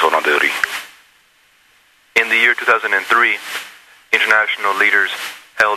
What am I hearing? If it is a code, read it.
Emergency Alert System (SAME) tones (header bursts), heard as harsh buzzing bursts, with speech. It is heard from 0.3 to 4.6 s.